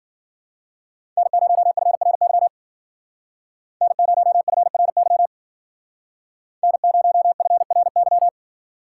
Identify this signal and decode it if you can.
Morse code, heard as a keyed tone, about 52 wpm, 700 Hz. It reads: N0FRY N0FRY N0FRY